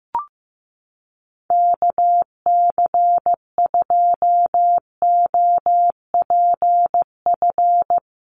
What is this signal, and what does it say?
Morse code, heard as a keyed tone, about 15 wpm, 700 Hz. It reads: KC2OPF
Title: USA Call Signs Repeated 2x 15wpm